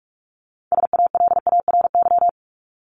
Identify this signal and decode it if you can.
Morse code, heard as a keyed tone, about 44 wpm, 700 Hz. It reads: SALARY
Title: Top 1751-2000 Words 45wpm